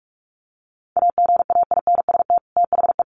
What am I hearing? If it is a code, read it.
Morse code, heard as a keyed tone, about 46 wpm, 700 Hz. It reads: AGAINST THE
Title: Sets of 2 Words 45wpm